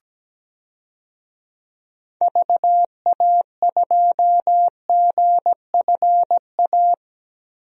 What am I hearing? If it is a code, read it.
Morse code, heard as a keyed tone, about 17 wpm, 700 Hz. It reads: VA2GFA
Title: DXCC-100 Call Signs - 6 Characters - 17wpm